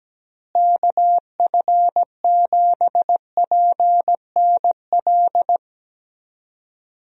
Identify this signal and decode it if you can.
Morse code, heard as a keyed tone, about 17 wpm, 700 Hz. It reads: KF7PNL
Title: USA Call Signs - 6 Characters - Repeated 2x - 17wpm